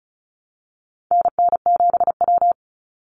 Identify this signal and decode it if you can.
Morse code, heard as a keyed tone, about 34 wpm, 700 Hz. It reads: NN7W
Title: Call Signs - 35wpm